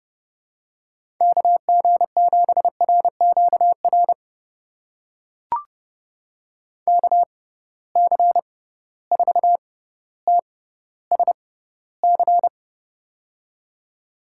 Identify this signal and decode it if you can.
Morse code, heard as a keyed tone, about 30 wpm, spaced as for 15 wpm, 700 Hz. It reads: KG7RQR KC4TSC
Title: USA Call Signs - 6 Characters - 6x Character Spacing - 30wpm